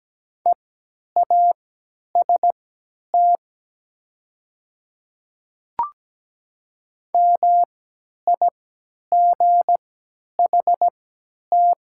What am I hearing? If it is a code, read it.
Morse code, heard as a keyed tone, about 17 wpm, spaced as for 10 wpm, 700 Hz. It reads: EAST MIGHT
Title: 3x Character Spacing - Top 500 Words - 17wpm